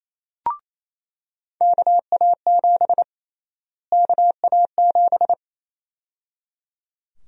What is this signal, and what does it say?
Morse code, heard as a keyed tone, about 28 wpm, 700 Hz. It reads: KA7 KA7
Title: USA Call Sign Prefixes - Repeated 2x - 28wpm